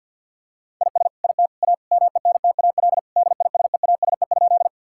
Morse code, heard as a keyed tone, about 50 wpm, 700 Hz. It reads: IS IT A MENTAL DISEASE?
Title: Sentences from Top 1000 Words - Review Part 6 - 50wpm